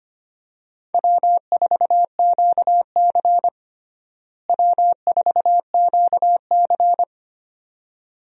Morse code, heard as a keyed tone, about 25 wpm, 700 Hz. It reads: W4QC W4QC
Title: USA Call Signs - 4 Characters - Repeated 2x - 25wpm